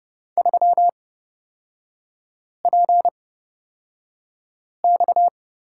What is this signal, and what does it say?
Morse code, heard as a keyed tone, about 29 wpm, spaced as for 5 wpm, 700 Hz. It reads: 3PX
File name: Lesson 37e - Letter x - Three Characters 30wpm